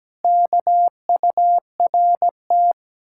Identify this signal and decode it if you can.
Morse code, heard as a keyed tone, about 18 wpm, 700 Hz. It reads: KURT
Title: Speed-Racing Common QSO Phrases Encore - 17wpm